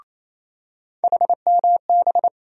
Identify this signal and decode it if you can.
Morse code, heard as a keyed tone, about 28 wpm, 700 Hz. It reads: HMB
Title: USA Call Sign Postfixes - 3 Letters 28wpm